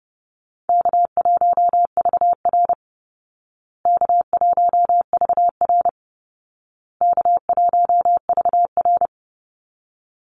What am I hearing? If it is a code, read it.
Morse code, heard as a keyed tone, about 30 wpm, 700 Hz. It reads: K1VR K1VR K1VR
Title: Easy Call Signs Repeated 3x - 30wpm